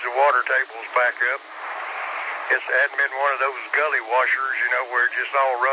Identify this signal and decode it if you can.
Emergency Alert System (SAME) tones, heard as harsh buzzing bursts, with speech. It is heard from 1.0 to 4.8 s.